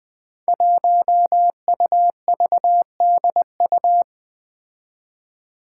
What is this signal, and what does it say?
Morse code, heard as a keyed tone, about 20 wpm, 700 Hz. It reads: 1UVDU
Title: Five Challenging Letter-Number Combinations 20wpm